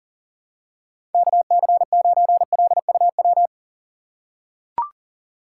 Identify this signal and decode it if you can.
Morse code, heard as a keyed tone, about 40 wpm, 700 Hz. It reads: KC9LUW
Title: USA Call Signs - 6 Characters - 4x Character Spacing - 40wpm